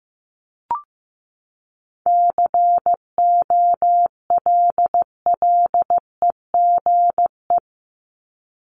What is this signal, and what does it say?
Morse code, heard as a keyed tone, about 15 wpm, 700 Hz. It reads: COLLEGE
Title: Top 1001-1250 Words 15wpm